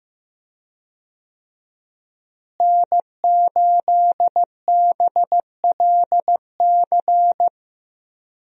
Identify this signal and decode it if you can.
Morse code, heard as a keyed tone, about 15 wpm, 700 Hz. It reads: N8BLC
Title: USA Call Signs - 5 Characters - Repeated 2x - 15wpm